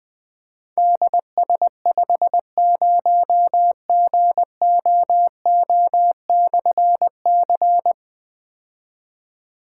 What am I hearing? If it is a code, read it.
Morse code, heard as a keyed tone, about 20 wpm, 700 Hz. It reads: DS50GOO/C
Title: Difficult Call Signs 20wpm